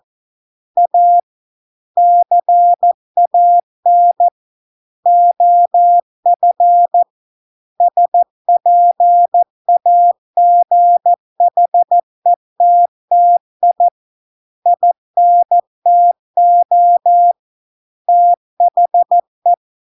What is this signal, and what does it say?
Morse code, heard as a keyed tone, about 14 wpm, 700 Hz. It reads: A CAN OF SPAGHETTI INTO THE